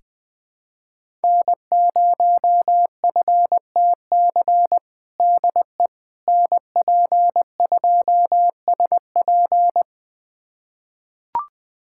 Morse code, heard as a keyed tone, about 21 wpm, 700 Hz. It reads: N0FTC DE NP2SP
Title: USA Callsign DE Callsign - 5 Characters - 20wpm